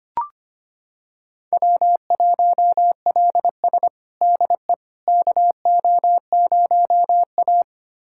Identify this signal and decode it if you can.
Morse code, heard as a keyed tone, about 25 wpm, 700 Hz. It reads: W1LS DE KO0A
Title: USA Callsign DE Callsign - 4 Characters - 25wpm